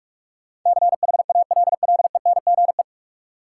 Morse code, heard as a keyed tone, about 41 wpm, 700 Hz. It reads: CHALLENGE
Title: Sentences from Top 1000 Words - Review Part 7 - New Word Drill - 45wpm